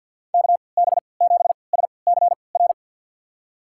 Morse code, heard as a keyed tone, about 51 wpm, spaced as for 28 wpm, 700 Hz. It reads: KB7SCR
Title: USA Call Signs - 3x Character Spacing - 50wpm